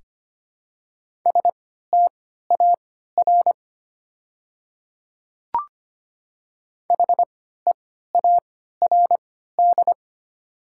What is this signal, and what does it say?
Morse code, heard as a keyed tone, about 25 wpm, spaced as for 14 wpm, 700 Hz. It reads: STAR HEARD
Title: 3x Character Spacing - Top 500 Words - 25wpm